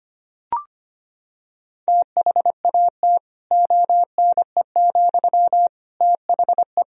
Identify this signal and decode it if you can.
Morse code, heard as a keyed tone, about 25 wpm, 700 Hz. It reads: THAT ONE, THE
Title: Sentences from Top 1000 Words - Review Part 1 - 25wpm